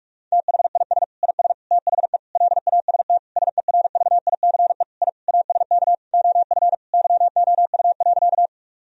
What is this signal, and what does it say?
Morse code, heard as a keyed tone, about 46 wpm, 700 Hz. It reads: THIS IS THE LAST SERVICE I ASK OF YOU.
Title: Sentences from Top 1000 Words - Review Part 4 - 45wpm